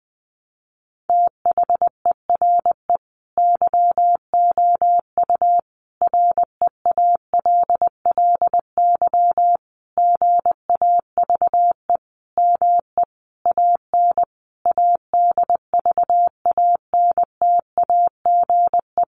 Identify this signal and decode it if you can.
Morse code, heard as a keyed tone, about 20 wpm, 700 Hz. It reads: THERE YOU REALLY GAVE ME AN ADVANTAGE
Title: Sentences from Top 900 Words 20wpm